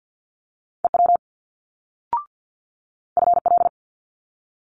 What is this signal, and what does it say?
Morse code, heard as a keyed tone, about 50 wpm, 700 Hz. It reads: EW FL
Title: Multi-letter Phonemes Quick Fire 50wpm